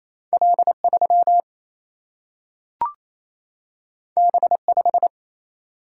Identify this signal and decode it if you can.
Morse code, heard as a keyed tone, about 28 wpm, 700 Hz. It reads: L 3 B 5
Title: Single-Letter Single-Number 28wpm